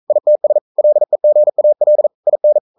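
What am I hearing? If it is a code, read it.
Morse code, heard as a keyed tone, about 42 wpm, 600 Hz. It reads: ITS LEGAL IN
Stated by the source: Morse code sending of an English novel at 42 wpm.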